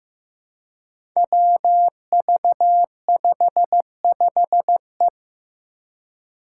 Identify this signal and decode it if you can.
Morse code, heard as a keyed tone, about 15 wpm, 700 Hz. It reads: WV55E
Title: Five Challenging Letter-Number Combinations 15wpm